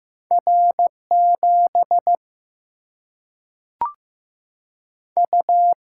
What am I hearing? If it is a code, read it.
Morse code, heard as a keyed tone, about 15 wpm, 700 Hz. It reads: R7 U
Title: Single-Letter Single-Number 15wpm